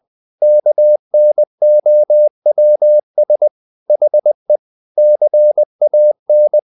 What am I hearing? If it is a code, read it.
Morse code, heard as a keyed tone, about 20 wpm, 600 Hz. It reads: KNOWS HE CAN
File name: the old man and the sea / the old man and the sea part 1 20wpm